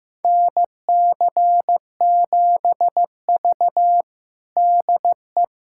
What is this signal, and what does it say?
Morse code, heard as a keyed tone, about 15 wpm, 700 Hz. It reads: NC7V DE
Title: USA Callsign DE Callsign - 4 Characters - 15wpm